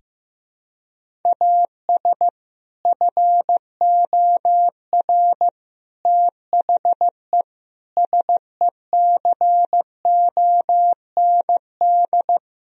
Morse code, heard as a keyed tone, about 16 wpm, 700 Hz. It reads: AS FOR THE SECOND
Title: Sentences from Top 1000 Words - Review Part 1 - 15wpm